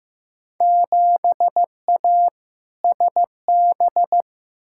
Morse code, heard as a keyed tone, about 15 wpm, 700 Hz. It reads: 7A SB
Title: Field Day Exchanges Encore - 15wpm